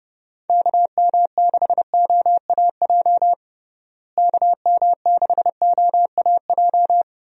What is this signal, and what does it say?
Morse code, heard as a keyed tone, about 30 wpm, 700 Hz. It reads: KM6OAJ KM6OAJ
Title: USA Call Signs - 6 Characters - Repeated 3x - 30wpm